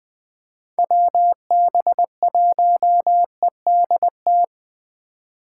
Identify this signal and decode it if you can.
Morse code, heard as a keyed tone, about 20 wpm, 700 Hz. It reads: WB1EDT
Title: Speed Racing - USA 2x3 Call Signs 17wpm